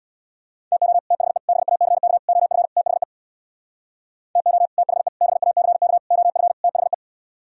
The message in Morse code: TO THE BACK OF THE TO THE BACK OF THE

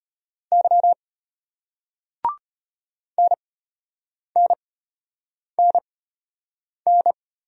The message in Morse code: Y NNNN

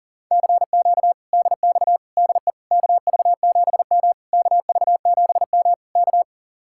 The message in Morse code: CQ DX DE KV7M KV7M K

